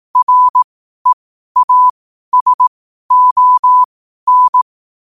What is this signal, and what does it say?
Morse code, heard as a keyed tone, about 18 wpm, spaced as for 12 wpm, 1000 Hz. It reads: REASON
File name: qotdfiles / 12wpm